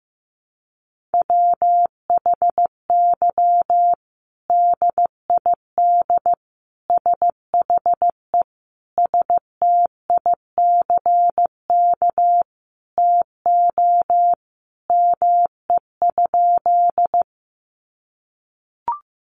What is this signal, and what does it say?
Morse code, heard as a keyed tone, about 15 wpm, 700 Hz. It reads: WHY DID SHE STICK TO ME?
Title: Sentences from Top 1200 Words 15wpm